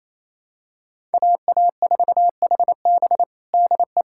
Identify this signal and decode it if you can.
Morse code, heard as a keyed tone, about 28 wpm, 700 Hz. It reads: AA4HB DE
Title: USA Callsign DE Callsign - 5 Characters - 28wpm